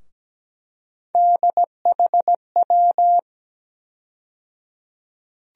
Morse code, heard as a keyed tone, about 17 wpm, 700 Hz. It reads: DHW
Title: USA Call Sign Postfixes - 3 Letters 17wpm